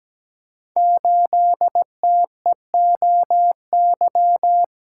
Morse code, heard as a keyed tone, about 17 wpm, 700 Hz. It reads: 8TEOY
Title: Code Groups - 5 Letter:Numbers - 17wpm